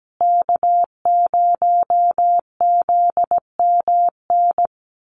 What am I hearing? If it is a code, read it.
Morse code, heard as a keyed tone, about 17 wpm, 700 Hz. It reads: K0ZMN
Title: USA Call Signs Repeated 2x 17wpm